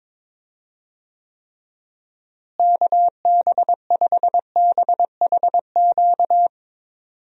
Morse code, heard as a keyed tone, about 22 wpm, 700 Hz. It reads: KB5BHQ
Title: USA Call Signs - 6 Characters - Repeated 3x - 22wpm